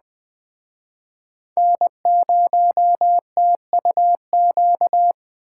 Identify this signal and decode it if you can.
Morse code, heard as a keyed tone, about 21 wpm, 700 Hz. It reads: N0TUQ